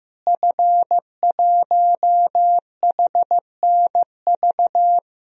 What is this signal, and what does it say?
Morse code, heard as a keyed tone, about 15 wpm, 700 Hz. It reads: F1HNV
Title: DXCC-100 Call Signs - 5 Characters - Repeated 2x - 15wpm